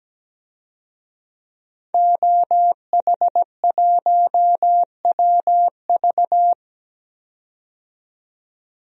Morse code, heard as a keyed tone, about 17 wpm, 700 Hz. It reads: OH1WV
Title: DXCC-100 Call Signs - 5 Characters - 17wpm